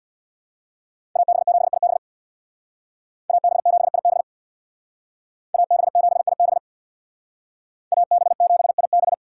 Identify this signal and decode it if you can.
Morse code, heard as a keyed tone, about 64 wpm, 700 Hz. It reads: AB7IB AB7IB AB7IB AB7IB